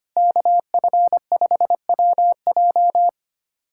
Morse code, heard as a keyed tone, about 25 wpm, 700 Hz. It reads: KF5WJ
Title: K1USN SST QSO Practice - 25wpm